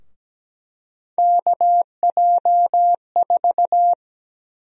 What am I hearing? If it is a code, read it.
Morse code, heard as a keyed tone, about 17 wpm, 700 Hz. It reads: KJ4
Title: USA Call Sign Prefixes - 17wpm